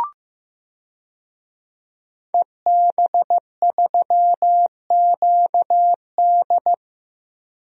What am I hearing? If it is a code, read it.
Morse code, heard as a keyed tone, about 15 wpm, 700 Hz. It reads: EB3QD
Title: DXCC-100 Call Signs - 5 Characters - Repeated 2x - 15wpm